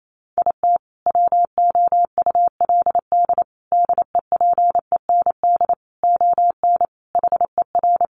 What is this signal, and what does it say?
Morse code, heard as a keyed tone, about 28 wpm, 700 Hz. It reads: IT WOULD DEPEND ON HER
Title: Sentences from Top 1300 Words 28wpm